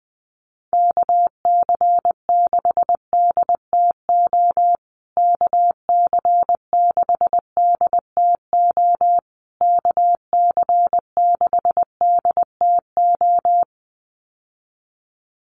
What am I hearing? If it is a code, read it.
Morse code, heard as a keyed tone, about 20 wpm, 700 Hz. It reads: KC6DTO KC6DTO KC6DTO
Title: USA Call Signs Repeated 3x 20wpm